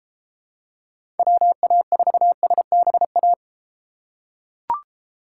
Morse code, heard as a keyed tone, about 32 wpm, 700 Hz. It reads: WA4SBA